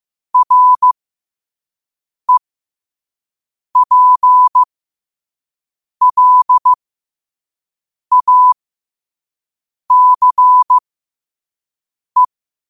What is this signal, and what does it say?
Morse code, heard as a keyed tone, about 15 wpm, spaced as for 5 wpm, 1000 Hz. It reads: REPLACE
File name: qotdfiles / 5wpm